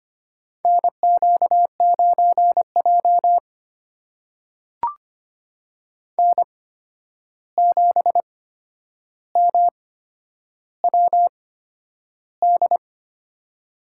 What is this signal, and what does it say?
Morse code, heard as a keyed tone, about 25 wpm, spaced as for 10 wpm, 700 Hz. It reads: NQ9J N7MWD
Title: USA Call Signs - 8x Character Spacing - 25wpm